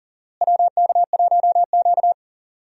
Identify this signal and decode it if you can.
Morse code, heard as a keyed tone, about 40 wpm, 700 Hz. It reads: WK1Q
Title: USA Call Signs - 4 Characters - Repeated 2x - 40wpm